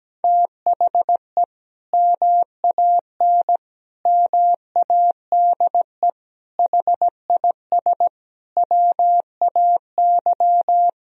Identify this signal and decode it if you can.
Morse code, heard as a keyed tone, about 17 wpm, 700 Hz. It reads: THE MAN MADE HIS WAY
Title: Sentences from Top 1000 Words - Review Part 2 - 17wpm